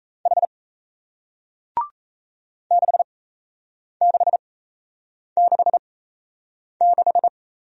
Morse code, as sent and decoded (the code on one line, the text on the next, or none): .... / -.... -.... -.... -....
H 6666